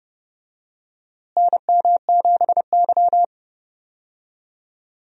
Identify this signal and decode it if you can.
Morse code, heard as a keyed tone, about 30 wpm, 700 Hz. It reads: NM7Y
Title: USA Call Signs - 4 Characters - Repeated 2x - 30wpm